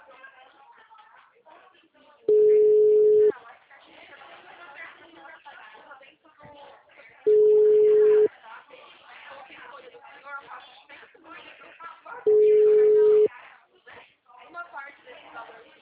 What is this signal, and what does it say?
Ringback tone (Europe, 425 Hz), with speech.